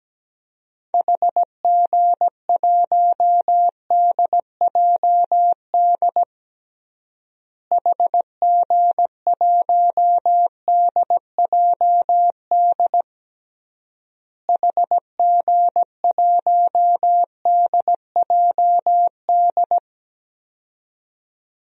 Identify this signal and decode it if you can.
Morse code, heard as a keyed tone, about 17 wpm, 700 Hz. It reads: HG1DJD HG1DJD HG1DJD
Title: DXCC-100 Call Signs - 6 Characters - Repeated 3x - 17wpm